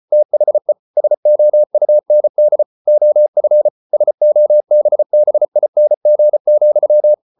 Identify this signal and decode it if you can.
Morse code, heard as a keyed tone, about 34 wpm, 600 Hz. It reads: THE SOUND OF SOBBING,